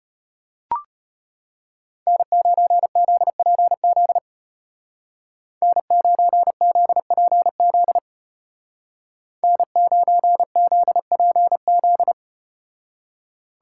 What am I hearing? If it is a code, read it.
Morse code, heard as a keyed tone, about 34 wpm, 700 Hz. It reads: N9ZPZ N9ZPZ N9ZPZ